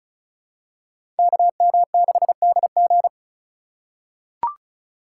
Morse code, heard as a keyed tone, about 34 wpm, 700 Hz. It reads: KM6DG
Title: USA Call Signs - 5 Characters - 4x Character Spacing - 35wpm